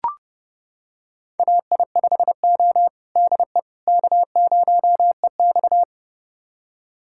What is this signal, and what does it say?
Morse code, heard as a keyed tone, about 30 wpm, 700 Hz. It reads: AI5O DE K0EX
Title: USA Callsign DE Callsign - 4 Characters - 30wpm